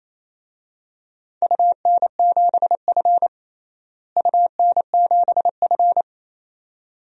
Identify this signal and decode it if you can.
Morse code, heard as a keyed tone, about 27 wpm, 700 Hz. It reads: UN7F UN7F
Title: DXCC-100 Call Signs - 4 Characters - Repeated 2x - 28wpm